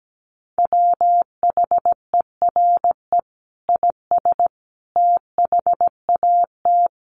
Morse code, heard as a keyed tone, about 17 wpm, 700 Hz. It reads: WHERE IS THAT
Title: Sentences from Top 1300 Words 17wpm